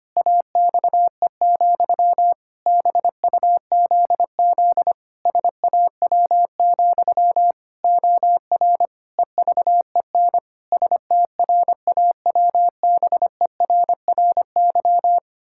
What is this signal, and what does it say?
Morse code, heard as a keyed tone, about 25 wpm, 700 Hz. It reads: AXE, BUZZ SAW, OR EVEN STRAWBERRY
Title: Pangram Challenge v2 - 25wpm